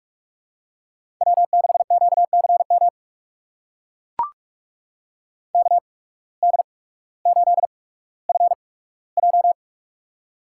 Morse code, heard as a keyed tone, about 44 wpm, spaced as for 19 wpm, 700 Hz. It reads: W6QCM KD8FJ
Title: USA Call Signs - 5 Characters - 8x Character Spacing - 45wpm